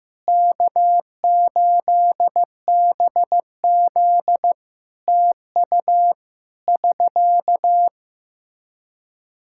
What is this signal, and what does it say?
Morse code, heard as a keyed tone, about 15 wpm, 700 Hz. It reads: K8BZ TU <SK>
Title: K1USN SST QSO Practice 15wpm